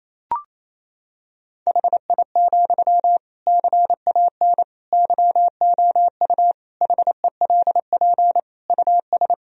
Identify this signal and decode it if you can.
Morse code, heard as a keyed tone, about 28 wpm, 700 Hz. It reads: HI, CAN YOU HELP US
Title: Sentences from Top 1000 Words - Review Part 3 - 28wpm